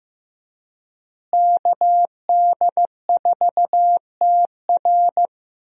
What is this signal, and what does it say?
Morse code, heard as a keyed tone, about 15 wpm, 700 Hz. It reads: KD4TR
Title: USA Call Signs - 5 Characters - Repeated 3x - 15wpm